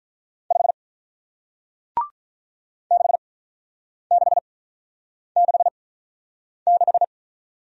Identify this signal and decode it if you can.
Morse code, heard as a keyed tone, about 47 wpm, spaced as for 9 wpm, 700 Hz. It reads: 5 6666